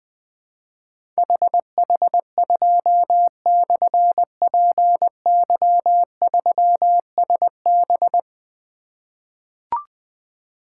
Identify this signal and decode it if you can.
Morse code, heard as a keyed tone, about 21 wpm, 700 Hz. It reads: HH2/PY3SB
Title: Difficult Call Signs 20wpm